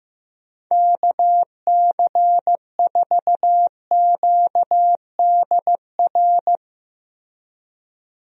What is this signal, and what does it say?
Morse code, heard as a keyed tone, about 15 wpm, 700 Hz. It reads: KC4QDR
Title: USA Call Signs - 6 Characters - Repeated 2x - 15wpm